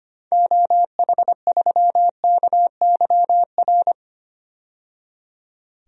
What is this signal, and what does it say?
Morse code, heard as a keyed tone, about 25 wpm, 700 Hz. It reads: OH3KYR
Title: DXCC-100 Call Signs - 6 Characters - 25wpm